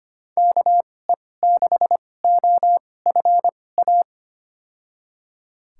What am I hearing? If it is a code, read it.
Morse code, heard as a keyed tone, about 25 wpm, spaced as for 18 wpm, 700 Hz. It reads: KE6OFA
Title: USA Call Signs - 6 Characters - 2x Character Spacing - 25wpm